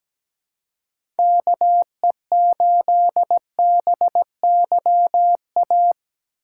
Morse code, heard as a keyed tone, about 17 wpm, 700 Hz. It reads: KE8BYA
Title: USA Call Signs - 6 Characters - 3x Character Spacing - 17wpm